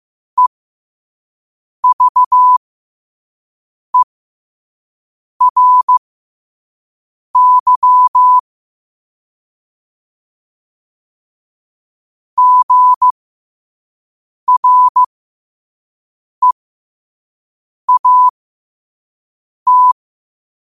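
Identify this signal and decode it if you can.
Morse code, heard as a keyed tone, about 15 wpm, spaced as for 5 wpm, 1000 Hz. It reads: EVERY GREAT
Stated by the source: Morse code at 5 wpm.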